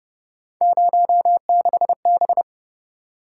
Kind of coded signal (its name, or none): Morse code